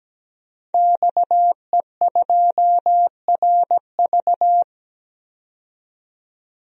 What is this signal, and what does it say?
Morse code, heard as a keyed tone, about 17 wpm, 700 Hz. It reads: XE2RV
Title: DXCC-100 Call Signs - 5 Characters - 17wpm